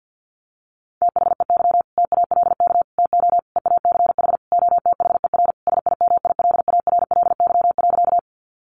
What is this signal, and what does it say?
Morse code, heard as a keyed tone, about 50 wpm, 700 Hz. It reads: THEY TALK TO EACH OTHER SIMILARLY.
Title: Sentences from Top 1400 Words 50wpm